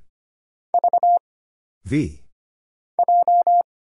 Morse code, heard as a keyed tone, about 25 wpm, 700 Hz. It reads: V J
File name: Single Character (Letter:Number:Punctuation:Prosign) - Mind-Melt - 25wpm